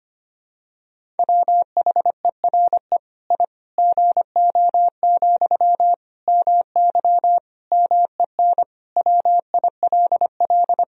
Morse code, heard as a keyed tone, about 25 wpm, 700 Hz. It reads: WHERE I GO, MY MEN WILL
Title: Sentences from Top 1000 Words - Review Part 1 - 25wpm